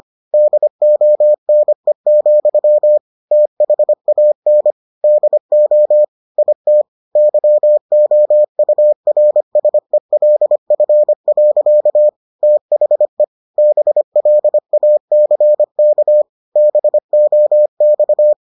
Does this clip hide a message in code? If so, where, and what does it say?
Yes: Morse code from 0.3 to 18.3 s, heard as a keyed tone, about 25 wpm, 600 Hz. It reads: DONE, THAN DO IT YOURSELF. THE BLACK BOX